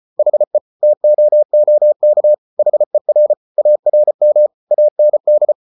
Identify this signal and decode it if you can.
Morse code, heard as a keyed tone, about 34 wpm, 600 Hz. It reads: HE TOOK HER ARM AND